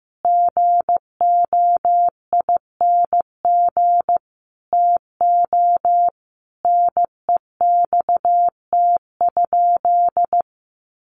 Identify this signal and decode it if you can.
Morse code, heard as a keyed tone, about 15 wpm, 700 Hz. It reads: GOING TO NEXT?
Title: Sentences from Top 1200 Words 15wpm